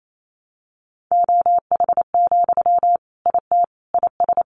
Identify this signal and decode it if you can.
Morse code, heard as a keyed tone, about 28 wpm, 700 Hz. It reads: OH, IT IS
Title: Sentences from Top 1000 Words 28wpm